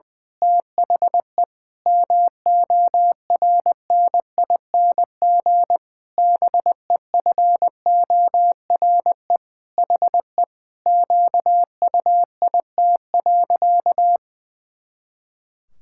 Morse code, heard as a keyed tone, about 21 wpm, 700 Hz. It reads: THE MORNING BEFORE HE QUIT.
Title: Pangram Challenge v2 - 20wpm